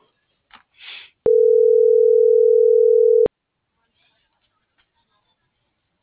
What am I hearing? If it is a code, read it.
Ringback tone (North America), 440 + 480 Hz.